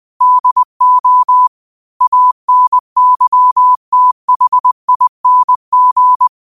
Morse code, heard as a keyed tone, about 20 wpm, 1000 Hz. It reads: DO ANYTHING
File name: qotdfiles / 20wpm